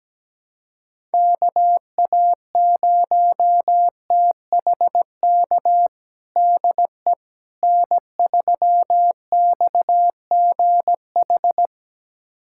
Morse code, heard as a keyed tone, about 17 wpm, 700 Hz. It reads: KA0THK DE N3XGH
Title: USA Callsign DE Callsign 17wpm